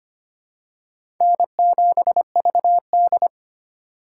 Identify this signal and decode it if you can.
Morse code, heard as a keyed tone, about 25 wpm, 700 Hz. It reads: N7VD
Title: USA Call Signs - 4 Characters - 2x Character Spacing - 25wpm